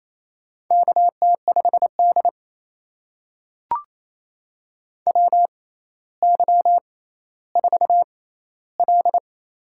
Morse code, heard as a keyed tone, about 28 wpm, spaced as for 14 wpm, 700 Hz. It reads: KT5D WY4L
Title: USA Call Signs - 6x Character Spacing - 28wpm